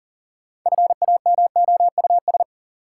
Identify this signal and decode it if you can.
Morse code, heard as a keyed tone, about 40 wpm, 700 Hz. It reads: FAMOUS
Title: Sentences from Top 1500 Words - New Word Drill - 40wpm